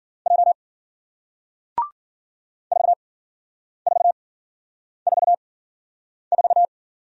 Morse code, heard as a keyed tone, about 53 wpm, spaced as for 9 wpm, 700 Hz. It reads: J 4444